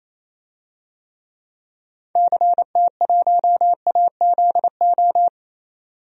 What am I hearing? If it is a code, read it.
Morse code, heard as a keyed tone, about 28 wpm, 700 Hz. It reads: CT1AZO
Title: DXCC-100 Call Signs - 6 Characters - 28wpm